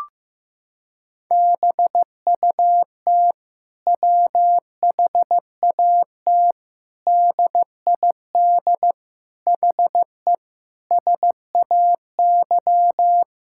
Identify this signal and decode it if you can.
Morse code, heard as a keyed tone, about 15 wpm, 700 Hz. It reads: BUT WHAT DID HE SAY
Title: Sentences from Top 1000 Words - Review Part 6 - 15wpm